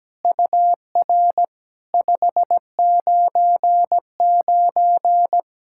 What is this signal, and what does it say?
Morse code, heard as a keyed tone, about 17 wpm, 700 Hz. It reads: UR 599
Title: SOTA QSO Practice 17wpm